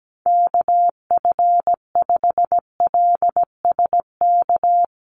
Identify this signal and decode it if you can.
Morse code, heard as a keyed tone, about 17 wpm, 700 Hz. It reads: KF5LSK
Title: USA Call Signs 17wpm